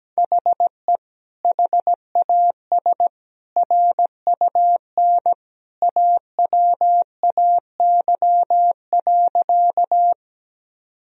Morse code, heard as a keyed tone, about 17 wpm, 700 Hz. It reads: HE HAS RUN AWAY.
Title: Sentences from Top 1000 Words - Review Part 1 - 17wpm